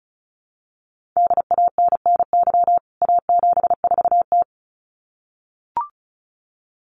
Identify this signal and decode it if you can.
Morse code, heard as a keyed tone, about 34 wpm, 700 Hz. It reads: DANNY A74T
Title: CWT Exchange 35wpm